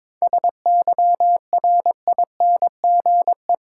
Morse code, heard as a keyed tone, about 22 wpm, 690 Hz. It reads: SYRINGE
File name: Difficult Words with Hints - 22wpm